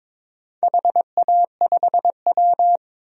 Morse code, heard as a keyed tone, about 22 wpm, 700 Hz. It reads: HA5W
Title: DXCC-100 Call Signs - 4 Characters - 22wpm